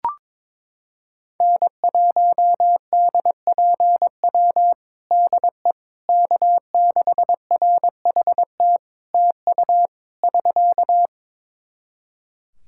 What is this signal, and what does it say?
Morse code, heard as a keyed tone, about 22 wpm, 700 Hz. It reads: N1DPW DE K6RHT TU <SK>
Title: Practice QSO - Exchange 4 - Ending - 22wpm